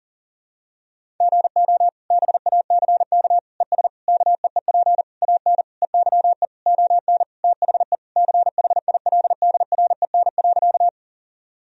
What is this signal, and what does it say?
Morse code, heard as a keyed tone, about 40 wpm, 700 Hz. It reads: GO BACK ES KEEP AN EYE ON THE CHILDREN.
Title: Sentences from Top 300 Words and Abbreviations 40wpm